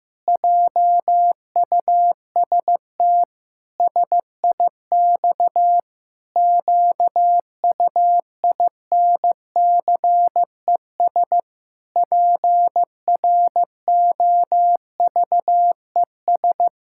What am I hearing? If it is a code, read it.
Morse code, heard as a keyed tone, about 15 wpm, 700 Hz. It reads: JUST SIX QUINCES PROVES
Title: Pangram Challenge v2 - 15wpm